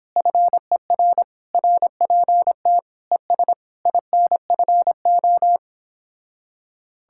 Morse code, heard as a keyed tone, about 29 wpm, 700 Hz. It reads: FER RPT ES INFO